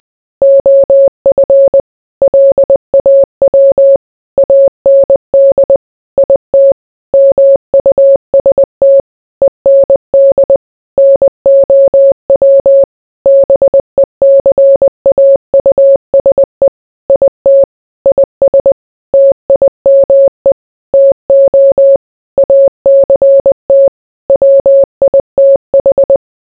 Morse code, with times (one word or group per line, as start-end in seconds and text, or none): 0.4-1.8 OF
2.2-4.0 LAW
4.4-5.8 AND
6.2-6.7 IT
7.1-9.0 MUST
9.4-10.6 END
11.0-12.8 NOW
13.3-16.7 BECAUSE
17.1-17.6 IT
18.1-18.7 IS
19.1-20.5 TIME
20.9-22.0 TO
22.4-23.9 ACT
24.3-26.2 WITH